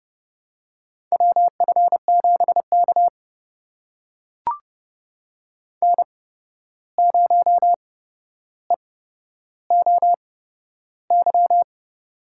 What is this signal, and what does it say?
Morse code, heard as a keyed tone, about 31 wpm, spaced as for 12 wpm, 700 Hz. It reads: WF7K N0EOY
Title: USA Call Signs - 8x Character Spacing - 30wpm